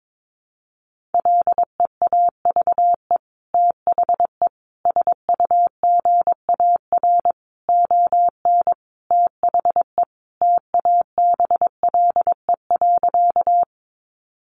Morse code, heard as a keyed tone, about 22 wpm, 700 Hz. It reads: LEAVE THE SUGAR ON THE TABLE.
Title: Sentences from Top 1200 Words 22wpm